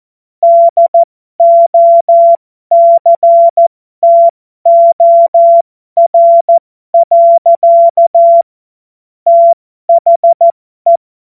Morse code, heard as a keyed tone, about 14 wpm, spaced as for 12 wpm, 680 Hz. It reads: DOCTOR. THE
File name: Morsekurs / Witze